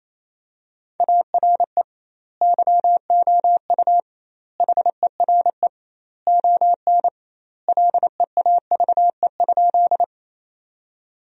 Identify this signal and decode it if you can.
Morse code, heard as a keyed tone, about 28 wpm, 700 Hz. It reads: ARE YOU HERE ON LEAVE?
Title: 2x Word-Spacing - Sentences from Top 400 Words - 28wpm